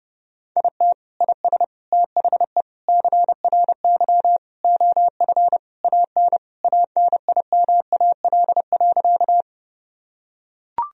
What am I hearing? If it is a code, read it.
Morse code, heard as a keyed tone, about 30 wpm, 700 Hz. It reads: IT IS THE CRY OF AN ANIMAL.
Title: Sentences from Top 1000 Words - Review Part 2 - 30wpm